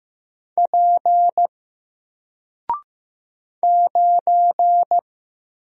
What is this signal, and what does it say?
Morse code, heard as a keyed tone, about 15 wpm, 700 Hz. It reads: P 9
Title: Single Character (Letter:Number:Punctuation:Prosign) Alt 15wpm